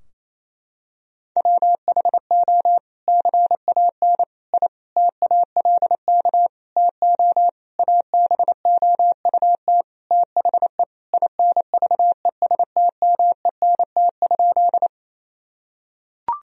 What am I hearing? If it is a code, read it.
Morse code, heard as a keyed tone, about 28 wpm, 700 Hz. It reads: WHO CAN I TALK TO ABOUT THE INVESTMENT?
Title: Sentences from Top 1000 Words - Review Part 4 - 28wpm